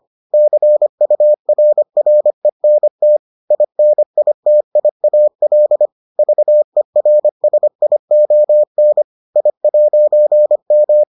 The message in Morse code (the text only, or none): CURRENT INITIAL VERSION I'M